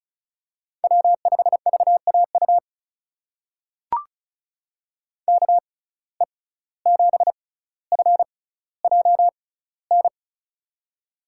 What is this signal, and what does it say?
Morse code, heard as a keyed tone, about 36 wpm, spaced as for 17 wpm, 700 Hz. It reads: W5VAU KE7FJN
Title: USA Call Signs - 6x Character Spacing - 35wpm